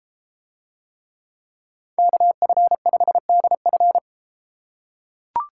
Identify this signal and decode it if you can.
Morse code, heard as a keyed tone, about 33 wpm, 700 Hz. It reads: KF5DF